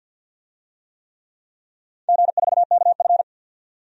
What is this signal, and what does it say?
Morse code, heard as a keyed tone, about 51 wpm, 700 Hz. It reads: G4KF
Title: DXCC-100 Call Signs - 4 Characters - 50wpm